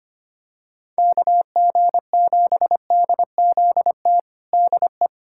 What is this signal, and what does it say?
Morse code, heard as a keyed tone, about 25 wpm, 700 Hz. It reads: KG7DZT DE
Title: USA Callsign DE Callsign - 6 Characters - 25wpm